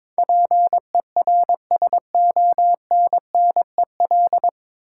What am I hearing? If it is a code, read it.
Morse code, heard as a keyed tone, about 22 wpm, 700 Hz. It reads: PERSONNEL
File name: practice / 22wpm / 9 Letter Words - First 3 Letters Given - 22wpm